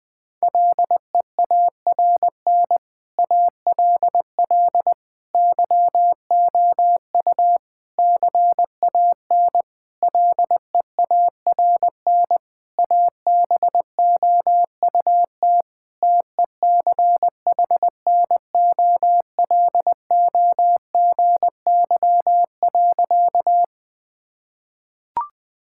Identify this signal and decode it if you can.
Morse code, heard as a keyed tone, about 21 wpm, 700 Hz. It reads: LEARN ALL YOU CAN LEARN ABOUT TECHNOLOGY.
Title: Sentences from Top 1000 Words - Review Part 5 - 20wpm